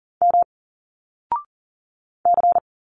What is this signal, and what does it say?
Morse code, heard as a keyed tone, about 40 wpm, 700 Hz. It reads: M C
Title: Single Letters 40wpm